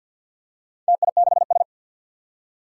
Morse code, heard as a keyed tone, about 47 wpm, 700 Hz. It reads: TI6S
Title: Four Challenging Letter-Number Combinations 50wpm